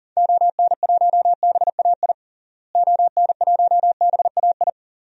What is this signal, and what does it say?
Morse code, heard as a keyed tone, about 37 wpm, 700 Hz. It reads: ON1BAI ON1BAI
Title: DXCC-100 Call Signs - 6 Characters - Repeated 2x - 40wpm